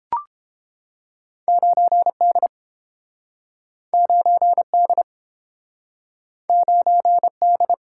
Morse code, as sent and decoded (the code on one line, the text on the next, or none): ----. -.. / ----. -.. / ----. -..
9D 9D 9D